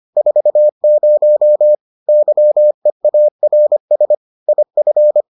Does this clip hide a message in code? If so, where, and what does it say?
Yes: Morse code from 0.2 to 5.2 s, heard as a keyed tone, about 25 wpm, 600 Hz. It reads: 40 YEARS IF